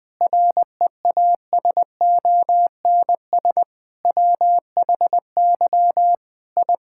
Morse code, heard as a keyed tone, about 21 wpm, 700 Hz. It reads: REASONS WHY I
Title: Sentences with 5-Word Phrases - 20wpm